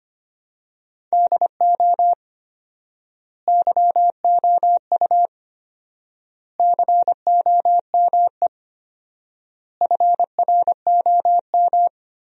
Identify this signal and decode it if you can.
Morse code, heard as a keyed tone, about 25 wpm, 700 Hz. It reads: DO YOU COME FROM
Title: 4x Word-Spacing - Sentences from Top 200 Words - 25wpm